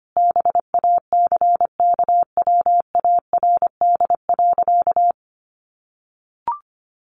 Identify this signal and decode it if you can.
Morse code, heard as a keyed tone, about 25 wpm, 700 Hz. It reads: BACKWARD.
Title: A Sound Of Thunder 25wpm